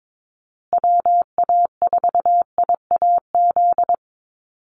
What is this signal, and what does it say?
Morse code, heard as a keyed tone, about 22 wpm, 700 Hz. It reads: WA4IAZ
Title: USA Call Signs Repeated 3x 22wpm